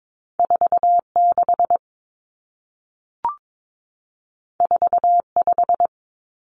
Morse code, heard as a keyed tone, about 22 wpm, 700 Hz. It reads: 46 45
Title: Common Cut Numbers - 2-Digit 22wpm